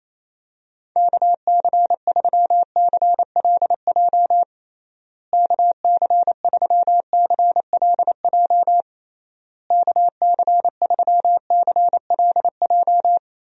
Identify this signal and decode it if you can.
Morse code, heard as a keyed tone, about 27 wpm, 700 Hz. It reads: KC3CLJ KC3CLJ KC3CLJ
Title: USA Call Signs - 6 Characters - Repeated 3x - 28wpm